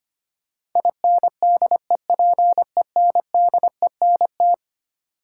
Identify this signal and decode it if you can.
Morse code, heard as a keyed tone, about 25 wpm, 700 Hz. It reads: INDEPENDENT
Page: morse-code-ninja-practice-25wpm